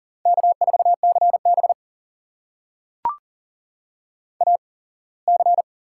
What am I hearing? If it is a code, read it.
Morse code, heard as a keyed tone, about 43 wpm, spaced as for 24 wpm, 700 Hz. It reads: K4CB AC
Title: USA Call Signs - 8x Character Spacing - 40wpm